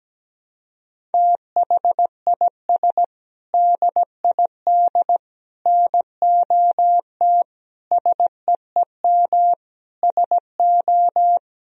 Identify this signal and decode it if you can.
Morse code, heard as a keyed tone, about 17 wpm, 700 Hz. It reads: THIS DID NOT SEEM SO
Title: Sentences from Top 1000 Words - Review Part 1 - 17wpm